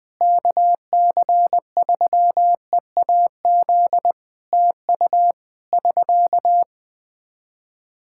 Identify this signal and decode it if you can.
Morse code, heard as a keyed tone, about 21 wpm, 700 Hz. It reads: KC3EAZ TU <SK>
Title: Practice QSO - Exchange 4 - Ending - 20wpm